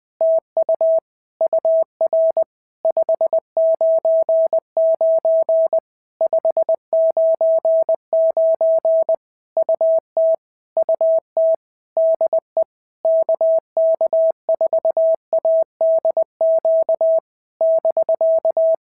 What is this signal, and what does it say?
Morse code, heard as a keyed tone, about 20 wpm, 650 Hz. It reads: TU UR 599 599 UT UT DE KK4ADQ <BK>
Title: POTA QSO Practice - 20wpm